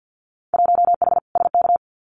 Morse code, heard as a keyed tone, about 50 wpm, 700 Hz. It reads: 1H SK
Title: Winter Field Day Exchanges 50wpm